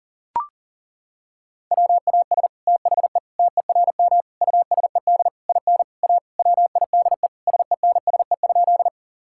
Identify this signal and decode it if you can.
Morse code, heard as a keyed tone, about 40 wpm, 700 Hz. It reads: WAS THE TERM USED IN A WIDE SENSE?
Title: Sentences from Top 1000 Words - Review Part 6 - 40wpm